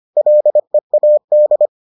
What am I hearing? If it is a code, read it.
Morse code, heard as a keyed tone, about 25 wpm, 600 Hz. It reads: LEAD